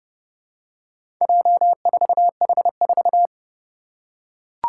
Morse code, heard as a keyed tone, about 32 wpm, 700 Hz. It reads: J4H4
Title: Four Challenging Letter-Number Combinations 30wpm